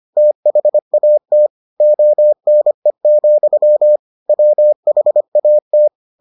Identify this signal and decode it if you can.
Morse code, heard as a keyed tone, about 25 wpm, 600 Hz. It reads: THAT ONE, WHAT